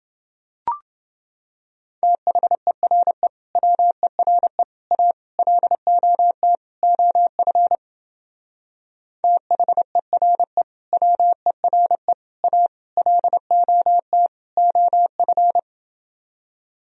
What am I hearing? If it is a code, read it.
Morse code, heard as a keyed tone, about 28 wpm, 700 Hz. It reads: THERE WERE A LOT OF THERE WERE A LOT OF